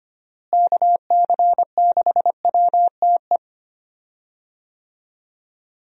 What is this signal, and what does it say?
Morse code, heard as a keyed tone, about 25 wpm, 700 Hz. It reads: KC6WTE
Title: USA Call Signs - 6 Characters - Repeated 2x - 25wpm